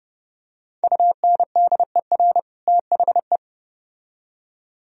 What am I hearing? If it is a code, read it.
Morse code, heard as a keyed tone, about 29 wpm, 700 Hz. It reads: UNDER THE